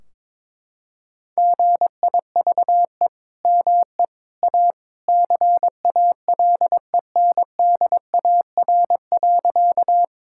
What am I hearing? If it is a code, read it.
Morse code, heard as a keyed tone, about 22 wpm, 700 Hz. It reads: GIVE ME A CALENDAR.
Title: Sentences from Top 1600 Words 22wpm